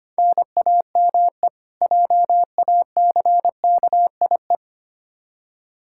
Morse code, heard as a keyed tone, about 25 wpm, 700 Hz. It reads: NAME JACKIE
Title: Practice QSO - Exchange 1 - Beginning - 25wpm